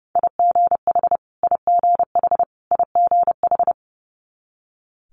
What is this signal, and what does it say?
Morse code, heard as a keyed tone, about 30 wpm, 700 Hz. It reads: IGH IGH IGH
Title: Multi-letter Phonemes Repeated 3x 30wpm